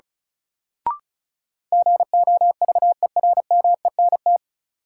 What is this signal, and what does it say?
Morse code, heard as a keyed tone, about 36 wpm, 700 Hz. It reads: GOVERMENT
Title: Sentences from Top 1300 Words - New Word Drill - 35wpm